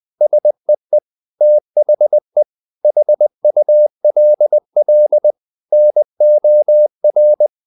Morse code, heard as a keyed tone, about 20 wpm, 600 Hz. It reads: SEE THE HULL NOR
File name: the old man and the sea part 2 20wpm